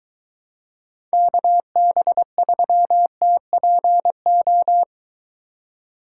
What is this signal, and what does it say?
Morse code, heard as a keyed tone, about 22 wpm, 700 Hz. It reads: KB3TPO